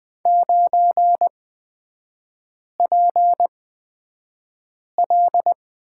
Morse code, heard as a keyed tone, about 20 wpm, spaced as for 5 wpm, 700 Hz. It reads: 9PL
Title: 243 Lesson 39e - Letter z - Three Characters 20wpm